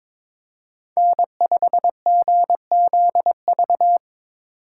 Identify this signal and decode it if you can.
Morse code, heard as a keyed tone, about 25 wpm, 700 Hz. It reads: N5GZV
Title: Speed Racing - USA Call Signs 22wpm